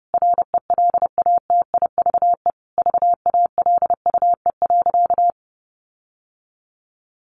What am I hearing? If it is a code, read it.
Morse code, heard as a keyed tone, about 30 wpm, 700 Hz. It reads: RELATIVE VALUE.
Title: Sentences from Top 1400 Words 30wpm